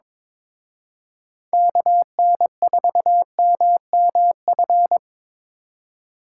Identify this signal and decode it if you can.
Morse code, heard as a keyed tone, about 25 wpm, 700 Hz. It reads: KN4MMF
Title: Speed Racing - USA 2x3 Call Signs 22wpm